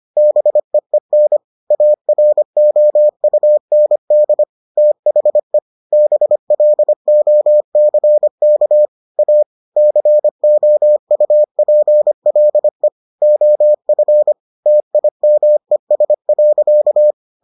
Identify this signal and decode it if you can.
Morse code, heard as a keyed tone, about 25 wpm, 600 Hz. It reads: BEEN AROUND THE BLOCK A COUPLE OF TIMES.